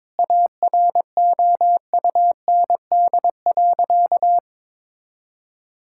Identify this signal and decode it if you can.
Morse code, heard as a keyed tone, about 21 wpm, 700 Hz. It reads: AROUND.
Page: morse-code-ninja-practice-22wpm